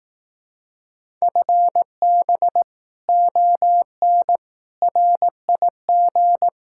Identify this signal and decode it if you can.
Morse code, heard as a keyed tone, about 18 wpm, 700 Hz. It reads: FB ON RIG
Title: Speed-Racing Common QSO Phrases Encore - 15wpm